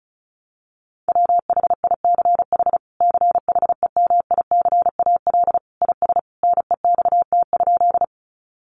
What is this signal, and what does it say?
Morse code, heard as a keyed tone, about 34 wpm, 700 Hz. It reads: WHICH CHEMICAL IS NEXT?
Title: Sentences from Top 1200 Words 35wpm